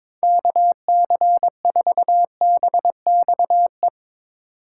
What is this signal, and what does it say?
Morse code, heard as a keyed tone, about 22 wpm, 700 Hz. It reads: KC4BXE
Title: Speed Racing - USA 2x3 Call Signs 22wpm